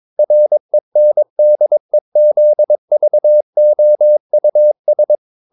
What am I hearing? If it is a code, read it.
Morse code, heard as a keyed tone, about 22 wpm, 600 Hz. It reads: RENDEZVOUS
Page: Romanen Never af Ken Follet på engelsk.